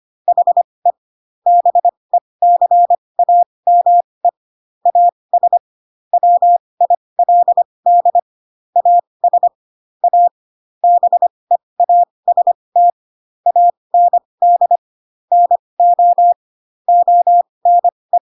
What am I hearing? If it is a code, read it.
Morse code, heard as a keyed tone, about 25 wpm, spaced as for 20 wpm, 700 Hz. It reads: HE BECAME AS WILD AS A BEAST AND NO ONE
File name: Pinocchio / Pinocchio ch2 15 25 ik1hge